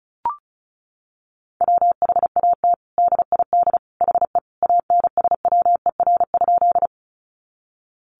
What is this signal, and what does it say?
Morse code, heard as a keyed tone, about 34 wpm, 700 Hz. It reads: WHAT DID HE ANSWER?
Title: Sentences from Top 300 Words 35wpm